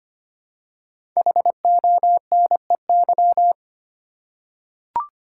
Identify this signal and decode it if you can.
Morse code, heard as a keyed tone, about 25 wpm, 700 Hz. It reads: HONEY
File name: 5 Letter Words - First 3 Letters Given - 25wpm